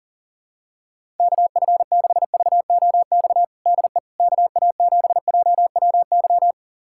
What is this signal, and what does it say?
Morse code, heard as a keyed tone, about 40 wpm, 700 Hz. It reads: KF6VOX DE KA7JWY
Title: USA Callsign DE Callsign - 6 Characters - 40wpm